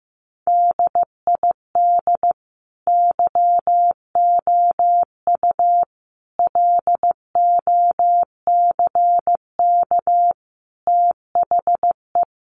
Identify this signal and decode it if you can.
Morse code, heard as a keyed tone, about 15 wpm, 700 Hz. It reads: DID YOU LOCK THE
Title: Sentences from Top 1300 Words 15wpm